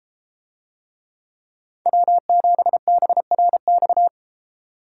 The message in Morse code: W7BRX

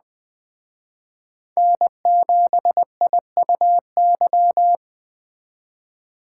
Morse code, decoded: N7IUY

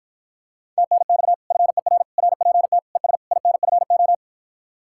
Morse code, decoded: TNX FER RPT ES INFO

